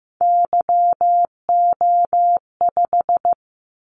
Morse code, heard as a keyed tone, about 15 wpm, 700 Hz. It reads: YO5
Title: Callsign Prefixes for Top 25 DXCC - 15wpm